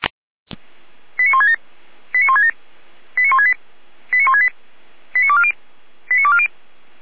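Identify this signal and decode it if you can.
Five-tone selective call (ZVEI), heard as quick runs of five tones, tones of about 70 ms. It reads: 89167 89167 89168 89168 89240 89240
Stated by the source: Received on an SDR, ZVEI five-tone selective call, channel 455.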